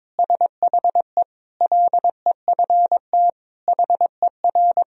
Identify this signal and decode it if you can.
Morse code, heard as a keyed tone, about 22 wpm, 700 Hz. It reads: SHE LEFT HER
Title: Sentences from Top 1000 Words - Review Part 3 - 22wpm